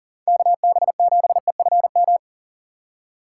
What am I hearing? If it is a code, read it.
Morse code, heard as a keyed tone, about 40 wpm, 700 Hz. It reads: KB7EFM
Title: USA Call Signs - 3x Character Spacing - 40wpm